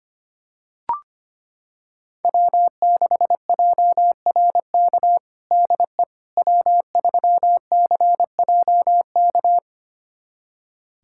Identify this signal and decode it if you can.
Morse code, heard as a keyed tone, about 25 wpm, 700 Hz. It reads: W6JRK DE W3CJK
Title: USA Callsign DE Callsign - 5 Characters - 25wpm